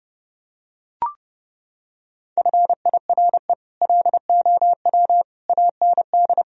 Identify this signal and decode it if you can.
Morse code, heard as a keyed tone, about 30 wpm, 700 Hz. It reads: FIRE LOW AND
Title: Sentences from Top 1000 Words - Review Part 1 - 30wpm